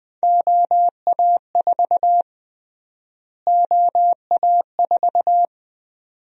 Morse code, heard as a keyed tone, about 20 wpm, 700 Hz. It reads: OA4 OA4
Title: USA Call Sign Prefixes - Repeated 2x - 20wpm